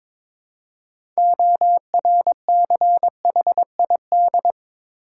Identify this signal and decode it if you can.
Morse code, heard as a keyed tone, about 22 wpm, 700 Hz. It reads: ORCHID
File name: Difficult Words with Hints - 22wpm